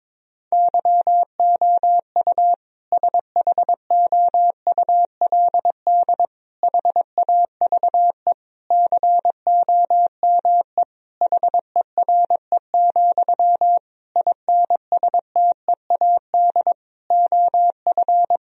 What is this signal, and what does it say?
Morse code, heard as a keyed tone, about 22 wpm, 700 Hz. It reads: YOU SHOULD HAVE COME HERE, INSTEAD OF
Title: Sentences from Top 1000 Words - Review Part 4 - 22wpm